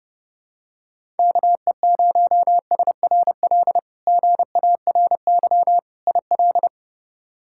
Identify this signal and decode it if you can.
Morse code, heard as a keyed tone, about 30 wpm, 700 Hz. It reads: KE0SRL GARY IL
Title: K1USN SST QSO Practice - 30wpm